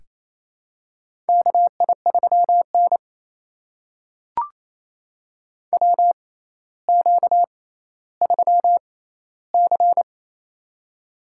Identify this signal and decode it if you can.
Morse code, heard as a keyed tone, about 28 wpm, spaced as for 14 wpm, 700 Hz. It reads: KI3N WQ3C
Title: USA Call Signs - 4 Characters - 6x Character Spacing - 28wpm